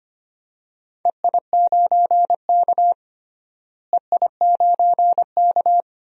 Morse code, heard as a keyed tone, about 25 wpm, 700 Hz. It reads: EI9K EI9K
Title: DXCC-100 Call Signs - 4 Characters - Repeated 3x - 25wpm